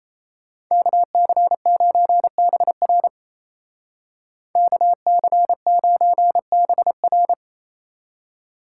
Morse code, heard as a keyed tone, about 35 wpm, 700 Hz. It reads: KC9BR KC9BR
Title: Speed Racing - USA 2x2 Call Signs 28wpm